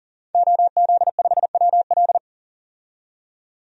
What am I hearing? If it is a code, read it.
Morse code, heard as a keyed tone, about 40 wpm, 700 Hz. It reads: OZ5WL
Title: DXCC-100 Call Signs - 5 Characters - 40wpm